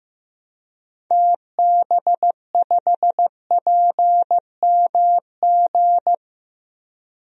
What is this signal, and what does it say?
Morse code, heard as a keyed tone, about 15 wpm, 700 Hz. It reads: TB5PMG
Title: DXCC-100 Call Signs - 6 Characters - Repeated 3x - 15wpm